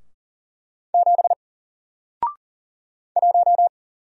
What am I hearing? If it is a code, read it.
Morse code, heard as a keyed tone, about 40 wpm, 700 Hz. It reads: 7 1
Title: Single Character (Letter:Number:Punctuation:Prosign) 40wpm